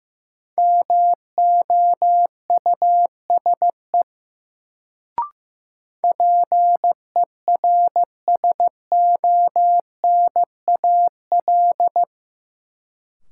Morse code, heard as a keyed tone, about 15 wpm, 700 Hz. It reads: MOUSE PERSONAL
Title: Sentences from Top 700 Words - New Word Drill - 15wpm